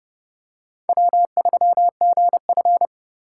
Morse code, heard as a keyed tone, about 30 wpm, 700 Hz. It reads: W3GF
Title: USA Call Signs - 4x Character Spacing - 30wpm